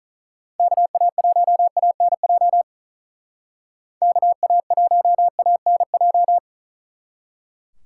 Morse code, heard as a keyed tone, about 46 wpm, 700 Hz. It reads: KA1ANJ KA1ANJ